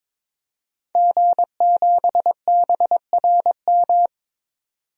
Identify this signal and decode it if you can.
Morse code, heard as a keyed tone, about 22 wpm, 700 Hz. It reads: G7BRM
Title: DXCC-100 Call Signs - 5 Characters - 22wpm